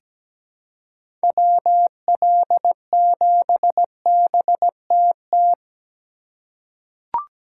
Morse code, heard as a keyed tone, about 17 wpm, 700 Hz. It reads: WL7BTT
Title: USA Call Signs - 6 Characters - 2x Character Spacing - 17wpm